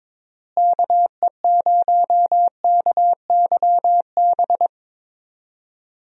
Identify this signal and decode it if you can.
Morse code, heard as a keyed tone, about 23 wpm, 700 Hz. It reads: KE0KYB
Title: USA Call Signs - 6 Characters - Repeated 3x - 22wpm